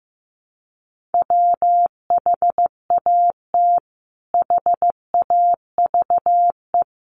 Morse code, heard as a keyed tone, about 15 wpm, 700 Hz. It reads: WHAT HAVE
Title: Sentences from Top 500 Words 15wpm